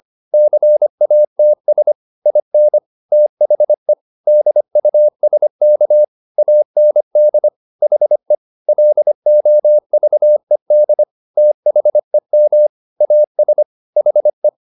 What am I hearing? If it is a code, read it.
Morse code, heard as a keyed tone, about 25 wpm, 600 Hz. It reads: CATS IN THE DUSK AND HE LOVED THEM AS HE
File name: the old man and the sea part 1 25wpm